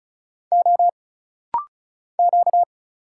Morse code, heard as a keyed tone, about 36 wpm, 700 Hz. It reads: O Q
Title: Single Letter-Number Alt - Rapid-Fire 35wpm